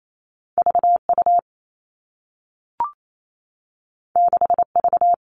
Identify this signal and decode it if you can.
Morse code, heard as a keyed tone, about 28 wpm, 700 Hz. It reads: VU 6V
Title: All Cut Numbers - 2-Digit 28wpm